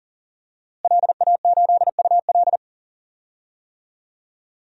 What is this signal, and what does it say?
Morse code, heard as a keyed tone, about 37 wpm, 700 Hz. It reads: LA8UL
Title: DXCC-100 Call Signs - 5 Characters - Repeated 2x - 40wpm